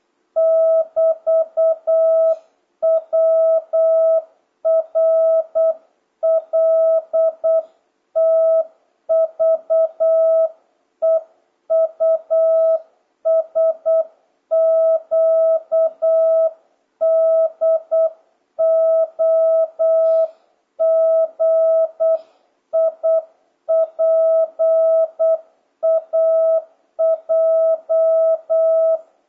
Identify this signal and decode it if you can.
Morse code, heard as a keyed tone, about 8 wpm, 650 Hz. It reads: <BT>WRLTVEUSQDOGIPAJ
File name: jh8cbh / 2026cw